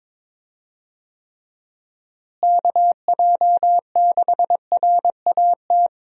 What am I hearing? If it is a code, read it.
Morse code, heard as a keyed tone, about 22 wpm, 700 Hz. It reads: KJ6RAT
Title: USA Call Signs - 6 Characters - Repeated 3x - 22wpm